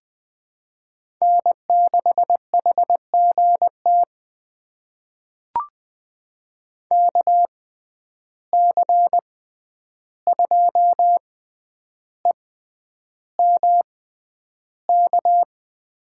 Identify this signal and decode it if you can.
Morse code, heard as a keyed tone, about 20 wpm, spaced as for 10 wpm, 700 Hz. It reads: N6HGT KC2EMK
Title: USA Call Signs - 6x Character Spacing - 20wpm